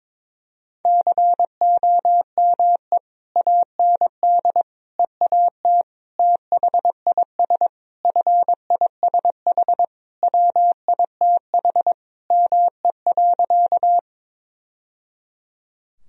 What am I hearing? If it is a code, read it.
Morse code, heard as a keyed tone, about 22 wpm, 700 Hz. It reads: COME AND EAT THIS FISH WITH ME.
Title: Sentences from Top 1000 Words - Review Part 1 - 22wpm